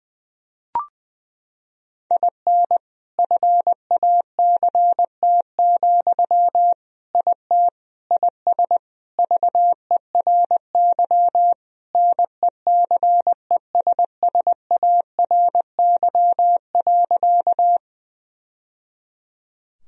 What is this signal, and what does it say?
Morse code, heard as a keyed tone, about 20 wpm, 700 Hz. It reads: IN FACT, IT IS VERY NECESSARY.
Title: Sentences from Top 1000 Words - Review Part 3 - 20wpm